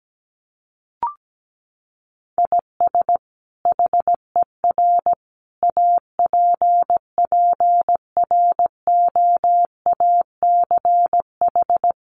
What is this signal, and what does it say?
Morse code, heard as a keyed tone, about 17 wpm, 700 Hz. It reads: IS HER APPROACH
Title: Sentences from Top 1500 Words 17wpm